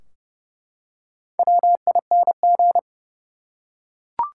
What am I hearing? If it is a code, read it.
Morse code, heard as a keyed tone, about 32 wpm, 700 Hz. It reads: WING
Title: Four Letter Words - 30wpm